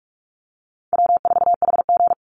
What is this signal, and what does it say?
Morse code, heard as a keyed tone, about 44 wpm, 700 Hz. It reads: W4HG
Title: Easy Call Signs - 45wpm